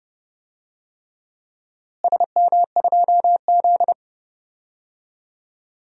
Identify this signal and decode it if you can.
Morse code, heard as a keyed tone, about 30 wpm, 700 Hz. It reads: SM2Z
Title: DXCC-100 Call Signs - 4 Characters - Repeated 3x - 30wpm